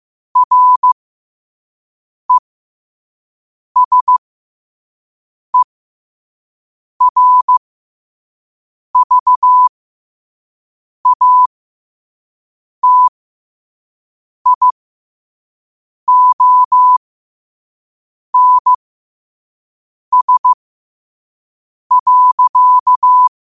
Morse code, heard as a keyed tone, about 15 wpm, spaced as for 5 wpm, 1000 Hz. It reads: RESERVATIONS.